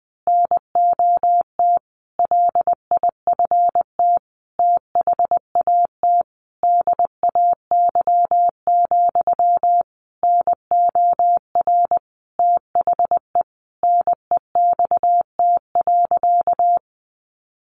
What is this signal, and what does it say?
Morse code, heard as a keyed tone, about 20 wpm, 700 Hz. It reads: NOT LIFT THAT DAY, NOR THE NEXT.
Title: Sentences from Top 1500 Words 20wpm